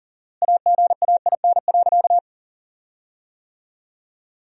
Morse code, heard as a keyed tone, about 40 wpm, 700 Hz. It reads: AGAIN.
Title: 4x Word-Spacing - Sentences from Top 500 Words - 40wpm